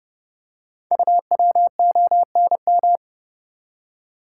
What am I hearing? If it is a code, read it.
Morse code, heard as a keyed tone, about 30 wpm, 700 Hz. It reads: UWONM
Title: Code Groups - 5 Letters - 30wpm